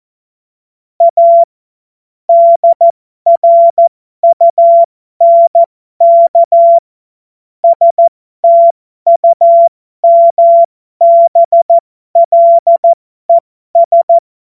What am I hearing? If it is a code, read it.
Morse code, heard as a keyed tone, about 14 wpm, spaced as for 12 wpm, 680 Hz. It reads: A DRUNK STUMBLES
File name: Morsekurs / Witze